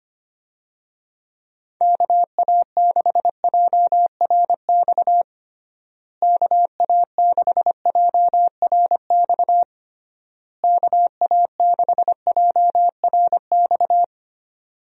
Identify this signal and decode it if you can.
Morse code, heard as a keyed tone, about 25 wpm, 700 Hz. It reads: KA6JRX KA6JRX KA6JRX
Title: USA Call Signs - 6 Characters - Repeated 3x - 25wpm